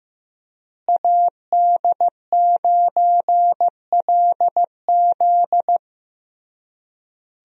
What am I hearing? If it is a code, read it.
Morse code, heard as a keyed tone, about 16 wpm, 700 Hz. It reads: AD9LZ
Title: USA Call Signs - 5 Characters - Repeated 3x - 15wpm